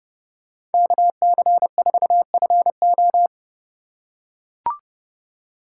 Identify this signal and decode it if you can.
Morse code, heard as a keyed tone, about 30 wpm, 700 Hz. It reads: KC4FO
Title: USA Call Signs - 5 Characters - 4x Character Spacing - 30wpm